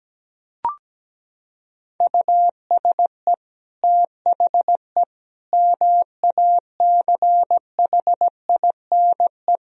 Morse code, heard as a keyed tone, about 17 wpm, 700 Hz. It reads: USE THE MACHINE
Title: Sentences from Top 1000 Words - Review Part 2 - 17wpm